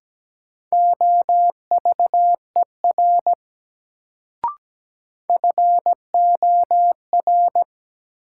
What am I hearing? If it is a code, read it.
Morse code, heard as a keyed tone, about 17 wpm, 700 Hz. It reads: OVER FOR
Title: Sets of 2 Words - New Word Drill - 17wpm